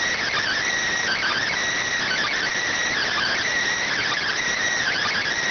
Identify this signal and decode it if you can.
Emergency Alert System (SAME) tones (header bursts), heard as harsh buzzing bursts. It is heard from 1.0 to 5.3 s.